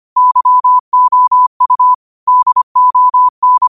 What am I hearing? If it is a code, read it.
Morse code, heard as a keyed tone, about 25 wpm, 1000 Hz. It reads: YOU DON